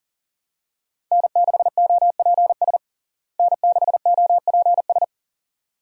Morse code, heard as a keyed tone, about 40 wpm, 700 Hz. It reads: N6OPS N6OPS
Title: USA Call Signs - 5 Characters - Repeated 2x - 40wpm